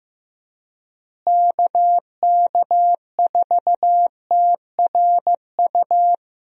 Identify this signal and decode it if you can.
Morse code, heard as a keyed tone, about 16 wpm, 700 Hz. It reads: KK4TRU
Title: USA Call Signs - 6 Characters - Repeated 3x - 15wpm